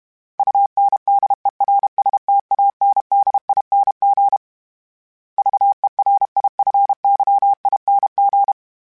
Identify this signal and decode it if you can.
Morse code, heard as a keyed tone, about 32 wpm, 800 Hz. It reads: UNDERSTANDING VERIFYING
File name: Special INGs-mp3s / Special INGs-mp3s / Sngl Ing @30